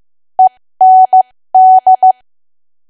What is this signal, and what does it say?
Morse code, heard as a keyed tone, about 15 wpm, spaced as for 13 wpm, 750 Hz. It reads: END
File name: cwa / W1AW-Level2 / 13WPM